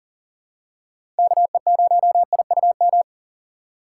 Morse code, heard as a keyed tone, about 37 wpm, 700 Hz. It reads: KE0IUM
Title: USA Call Signs - 6 Characters - 3x Character Spacing - 40wpm